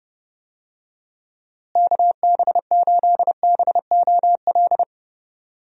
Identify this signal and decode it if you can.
Morse code, heard as a keyed tone, about 31 wpm, 700 Hz. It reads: KB8BOL
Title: USA Call Signs - 6 Characters - Repeated 2x - 30wpm